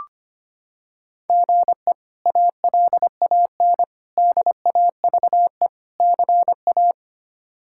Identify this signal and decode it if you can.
Morse code, heard as a keyed tone, about 25 wpm, 700 Hz. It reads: GE ALAN DAVE CA
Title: K1USN SST QSO Practice - 25wpm